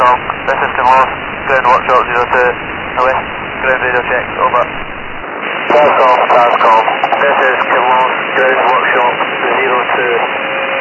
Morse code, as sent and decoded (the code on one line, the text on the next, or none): none